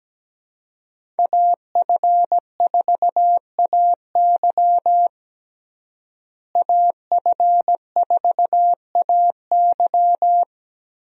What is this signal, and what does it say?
Morse code, heard as a keyed tone, about 17 wpm, 700 Hz. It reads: AF4AY AF4AY
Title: USA Call Signs - 5 Characters - Repeated 3x - 17wpm